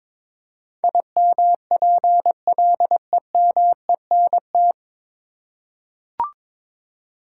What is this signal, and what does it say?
Morse code, heard as a keyed tone, about 22 wpm, 700 Hz. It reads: IMPLEMENT
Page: Morse Code Ninja practice 22wpm